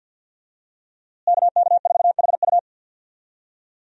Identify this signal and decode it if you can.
Morse code, heard as a keyed tone, about 47 wpm, 700 Hz. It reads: KK4HU
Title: USA Call Signs - 5 Characters - Repeated 3x - 50wpm